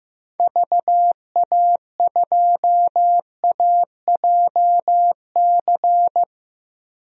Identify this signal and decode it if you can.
Morse code, heard as a keyed tone, about 15 wpm, 700 Hz. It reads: VA2AJC
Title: DXCC-100 Call Signs - 6 Characters - 15wpm